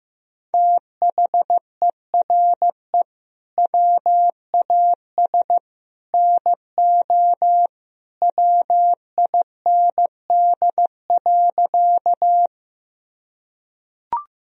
Morse code, heard as a keyed tone, about 15 wpm, 700 Hz. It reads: THERE WAS NO WIND.
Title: Sentences from Top 1000 Words - Review Part 2 - 15wpm